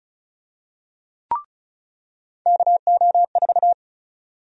Morse code, heard as a keyed tone, about 36 wpm, 700 Hz. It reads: KO4
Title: USA Call Sign Prefixes - 35wpm